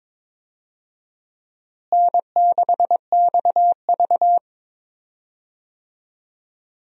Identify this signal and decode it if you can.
Morse code, heard as a keyed tone, about 22 wpm, 700 Hz. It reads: N6XV
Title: USA Call Signs - 4 Characters - Repeated 3x - 22wpm